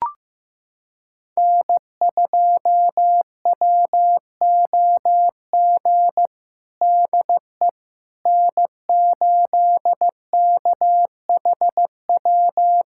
Morse code, heard as a keyed tone, about 15 wpm, 700 Hz. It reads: N2WOG DE N8KHW
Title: USA Callsign DE Callsign - 5 Characters - 15wpm